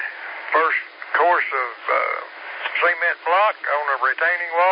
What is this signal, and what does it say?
Emergency Alert System (SAME) tones, heard as harsh buzzing bursts, with speech. It is heard from 0.7 to 4.4 s.